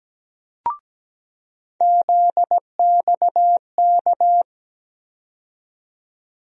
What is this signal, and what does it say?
Morse code, heard as a keyed tone, about 17 wpm, 700 Hz. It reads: ZXK
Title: USA Call Sign Postfixes - 3 Letters 17wpm